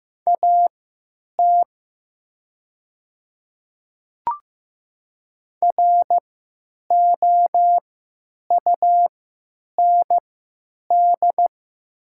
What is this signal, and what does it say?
Morse code, heard as a keyed tone, about 15 wpm, spaced as for 9 wpm, 700 Hz. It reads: AT ROUND
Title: 3x Character Spacing - Top 300 Words - 15wpm